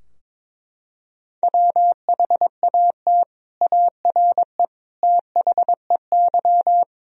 Morse code, heard as a keyed tone, about 22 wpm, 700 Hz. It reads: WHAT ARE THEY
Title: Sentences from Top 1000 Words - Review Part 3 - 22wpm